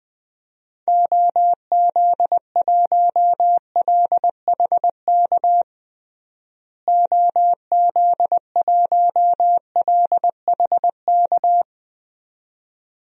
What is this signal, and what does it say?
Morse code, heard as a keyed tone, about 20 wpm, 700 Hz. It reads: OZ1LHK OZ1LHK
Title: DXCC-100 Call Signs - 6 Characters - Repeated 2x - 20wpm